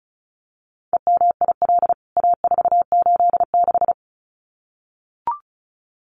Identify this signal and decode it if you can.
Morse code, heard as a keyed tone, about 34 wpm, 700 Hz. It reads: EMIL A486
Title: CWT Exchange 35wpm